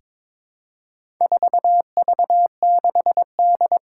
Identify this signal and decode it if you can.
Morse code, heard as a keyed tone, about 22 wpm, 700 Hz. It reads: 4V6D
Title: Four Challenging Letter-Number Combinations 22wpm